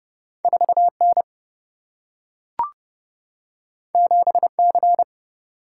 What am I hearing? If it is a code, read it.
Morse code, heard as a keyed tone, about 30 wpm, 700 Hz. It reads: 4 N 7 C
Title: Single-Letter Single-Number 30wpm